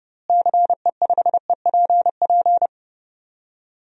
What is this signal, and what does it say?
Morse code, heard as a keyed tone, about 30 wpm, 700 Hz. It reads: CE5EPP
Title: DXCC-100 Call Signs - 6 Characters - Repeated 3x - 30wpm